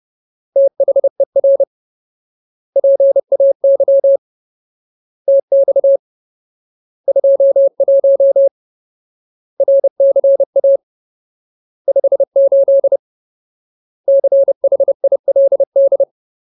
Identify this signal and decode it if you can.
Morse code, heard as a keyed tone, about 30 wpm, 550 Hz. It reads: THER PAY TX 21 RCA 58 CHILD
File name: Ultimate Word List / 30WPM Ultimate